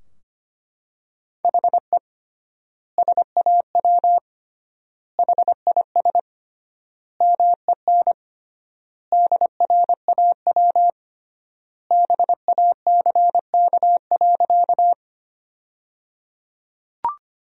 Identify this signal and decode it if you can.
Morse code, heard as a keyed tone, about 25 wpm, 700 Hz. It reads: HE SAW HIS MEN DRAW BACK.
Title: 3x Word-Spacing - Sentences from Top 300 Words - 25wpm